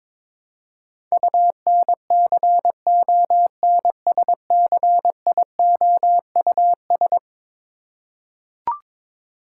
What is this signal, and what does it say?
Morse code, heard as a keyed tone, about 22 wpm, 700 Hz. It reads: UNCONSCIOUS
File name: practice / 22wpm / Eleven Letter Words - 22wpm